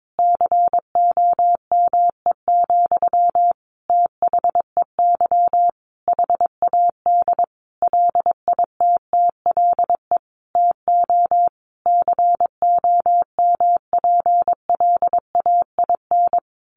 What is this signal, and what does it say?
Morse code, heard as a keyed tone, about 22 wpm, 700 Hz. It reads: COME, THEY HAD LITTLE TO COMPLAIN
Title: Sentences from Top 1300 Words 22wpm